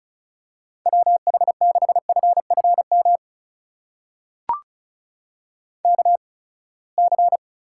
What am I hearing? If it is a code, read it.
Morse code, heard as a keyed tone, about 36 wpm, spaced as for 24 wpm, 700 Hz. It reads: WH6FFM KC
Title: USA Call Signs - 6 Characters - 8x Character Spacing - 35wpm